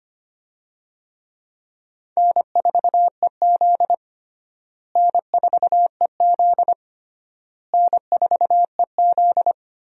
Morse code, heard as a keyed tone, about 25 wpm, 700 Hz. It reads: N4EZ N4EZ N4EZ
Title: USA Call Signs - 4 Characters - Repeated 3x - 25wpm